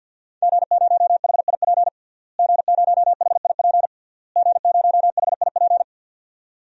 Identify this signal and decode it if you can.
Morse code, heard as a keyed tone, about 52 wpm, 700 Hz. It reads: G0HIP G0HIP G0HIP
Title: DXCC-100 Call Signs - 5 Characters - Repeated 3x - 50wpm